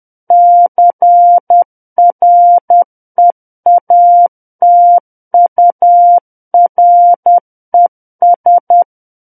Morse code, heard as a keyed tone, about 10 wpm, 700 Hz. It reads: CREATURES
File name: Book One / 01-Chapter 1 - 10 WPM